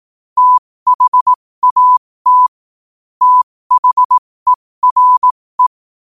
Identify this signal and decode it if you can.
Morse code, heard as a keyed tone, about 18 wpm, spaced as for 15 wpm, 1000 Hz. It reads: THAT THERE
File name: qotdfiles / 15wpm